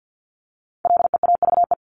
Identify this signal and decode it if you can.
Morse code, heard as a keyed tone, about 50 wpm, 700 Hz. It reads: LEAVE
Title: Top 500 Words 50wpm